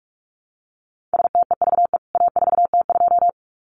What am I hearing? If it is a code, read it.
Morse code, heard as a keyed tone, about 44 wpm, 700 Hz. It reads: STEVE A4T2
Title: CWT Exchange 45wpm